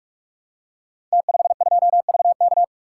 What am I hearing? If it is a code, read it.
Morse code, heard as a keyed tone, about 44 wpm, 700 Hz. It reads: T52VK
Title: Code Groups - 5 Letter:Numbers - 45wpm